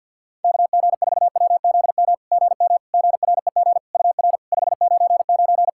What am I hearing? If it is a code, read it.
Morse code, heard as a keyed tone, about 51 wpm, 700 Hz. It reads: KG4WZM GM GREG UR 599
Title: SOTA QSO Practice 50wpm